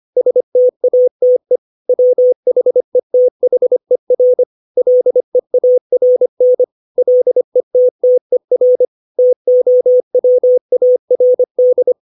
Morse code, heard as a keyed tone, about 25 wpm, 500 Hz. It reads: STATE WHETHER LEARN LETTER TOWARD
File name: 632 words / 632 Words 25WPM